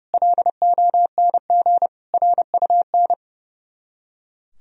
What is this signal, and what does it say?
Morse code, heard as a keyed tone, about 38 wpm, 700 Hz. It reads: LONG RUN